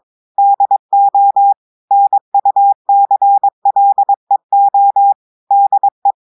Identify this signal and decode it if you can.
Morse code, heard as a keyed tone, about 22 wpm, 800 Hz. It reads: DO NUCLEO DE
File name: audios / portugais / portugais texte en clair 1 vitesse 22 mots minutes